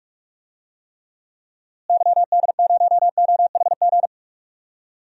Morse code, heard as a keyed tone, about 42 wpm, 700 Hz. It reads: YD0OHG
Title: DXCC-100 Call Signs - 6 Characters - Repeated 2x - 45wpm